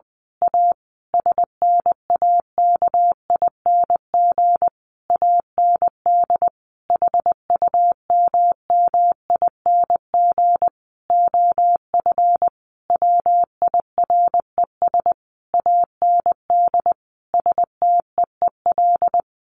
Morse code, heard as a keyed tone, about 20 wpm, 700 Hz. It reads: A SNAKING AND HUMMING OF WIRES AND STEEL
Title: A Sound Of Thunder 20wpm